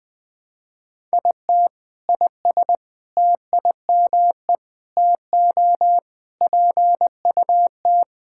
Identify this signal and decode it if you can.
Morse code, heard as a keyed tone, about 20 wpm, 700 Hz. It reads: IT IS TIME TO PUT
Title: Sentences with 7-Word Phrases - 20wpm